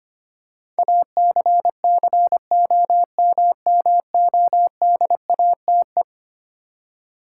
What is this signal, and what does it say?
Morse code, heard as a keyed tone, about 25 wpm, 700 Hz. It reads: ACCOMMODATE
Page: morse-code-ninja-practice-25wpm